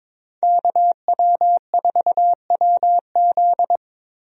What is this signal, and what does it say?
Morse code, heard as a keyed tone, about 22 wpm, 700 Hz. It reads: KW4WZ
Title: USA Call Signs - 5 Characters - Repeated 2x - 22wpm